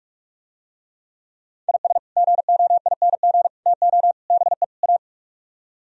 Morse code, heard as a keyed tone, about 48 wpm, 700 Hz. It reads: IS GOING TO BE A